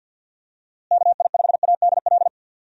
Morse code, heard as a keyed tone, about 52 wpm, 700 Hz. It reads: KI5ADL
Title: USA Call Signs - 6 Characters - Repeated 3x - 50wpm